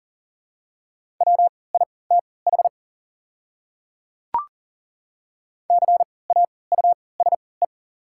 Morse code, heard as a keyed tone, about 40 wpm, spaced as for 23 wpm, 690 Hz. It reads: WITH CAUSE
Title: 3x Character Spacing - Top 200 Words - 40wpm